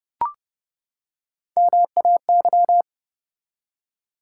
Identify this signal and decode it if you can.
Morse code, heard as a keyed tone, about 30 wpm, 700 Hz. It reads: MAY
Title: Three Letter Words - 30wpm